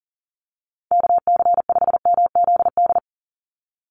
Morse code, heard as a keyed tone, about 40 wpm, 700 Hz. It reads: KC5MZD
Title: USA Call Signs 40wpm